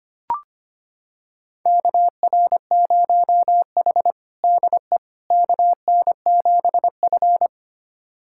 Morse code, heard as a keyed tone, about 25 wpm, 700 Hz. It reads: KR0H DE KN7F
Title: USA Callsign DE Callsign - 4 Characters - 25wpm